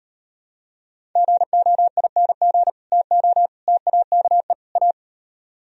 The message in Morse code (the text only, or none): GOING TO TAKE A